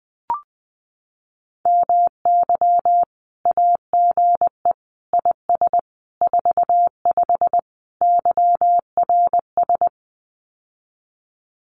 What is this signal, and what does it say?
Morse code, heard as a keyed tone, about 20 wpm, 700 Hz. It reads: MY AGE IS 45 YRS
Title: Common QSO Phrases Encore 20wpm